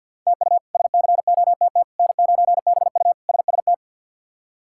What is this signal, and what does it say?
Morse code, heard as a keyed tone, about 53 wpm, 690 Hz. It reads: TU SCOTT N9BU SST
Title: K1USN SST QSO Practice - 50wpm